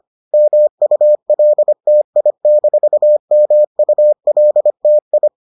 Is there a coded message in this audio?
Yes: Morse code from 0.3 to 5.3 s, heard as a keyed tone, about 25 wpm, 600 Hz. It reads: MULTI-MULTI